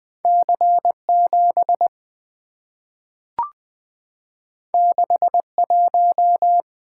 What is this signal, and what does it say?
Morse code, heard as a keyed tone, about 20 wpm, 690 Hz. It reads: C7 61
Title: Two Letter-Number Combinations - 20wpm